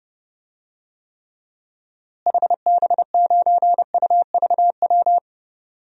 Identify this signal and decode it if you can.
Morse code, heard as a keyed tone, about 30 wpm, 700 Hz. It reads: HB9UVW
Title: DXCC-100 Call Signs - 6 Characters - Repeated 2x - 30wpm